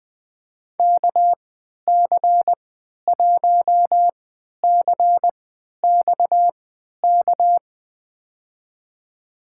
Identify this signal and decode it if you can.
Morse code, heard as a keyed tone, about 20 wpm, spaced as for 11 wpm, 700 Hz. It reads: KC1CXK
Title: USA Call Signs - 6 Characters - 3x Character Spacing - 20wpm